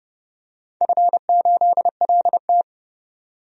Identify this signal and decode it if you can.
Morse code, heard as a keyed tone, about 30 wpm, 700 Hz. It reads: F8LT
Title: DXCC-100 Call Signs - 4 Characters - 30wpm